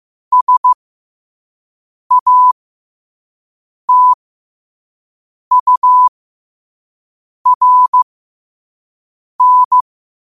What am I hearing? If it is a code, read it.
Morse code, heard as a keyed tone, about 15 wpm, spaced as for 5 wpm, 1000 Hz. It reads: SATURN
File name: qotdfiles / 5wpm